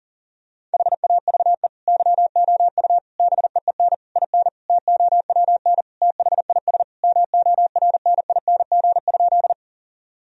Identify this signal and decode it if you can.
Morse code, heard as a keyed tone, about 40 wpm, 700 Hz. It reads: HAVE YOU BEEN IN TOWN THIS MORNING?
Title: Sentences from Top 1000 Words - Review Part 2 - 40wpm